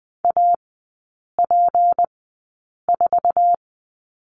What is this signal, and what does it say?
Morse code, heard as a keyed tone, about 20 wpm, spaced as for 8 wpm, 690 Hz. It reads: AP4
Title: 166 Lesson 27e - Number 7 - Three Characters 20wpm